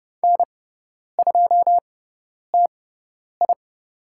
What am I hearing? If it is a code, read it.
Morse code, heard as a keyed tone, about 30 wpm, spaced as for 10 wpm, 700 Hz. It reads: N2TI
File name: Lesson 18g - Letter m - Call Signs 30wpm